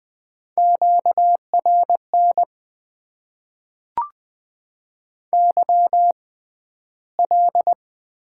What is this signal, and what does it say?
Morse code, heard as a keyed tone, about 20 wpm, spaced as for 12 wpm, 700 Hz. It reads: QRN YL
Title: Top 90 QSO Elements - 6x Character Spacing - 20wpm